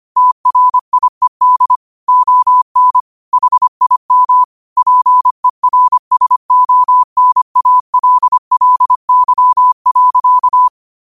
Morse code, heard as a keyed tone, about 25 wpm, 1000 Hz. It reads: TRIED ON HIM PERSONALLY.
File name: qotdfiles / 25wpm